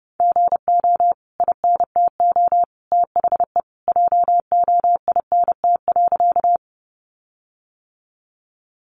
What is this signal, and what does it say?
Morse code, heard as a keyed tone, about 30 wpm, 700 Hz. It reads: GO INTO THE JOINT.
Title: Sentences from Top 1300 Words 30wpm